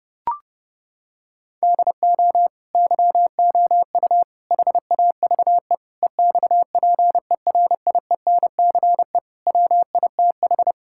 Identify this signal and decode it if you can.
Morse code, heard as a keyed tone, about 30 wpm, 700 Hz. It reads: DO YOU HAVE EXPERIENCE WITH
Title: Sentences from Top 1000 Words - Review Part 5 - 30wpm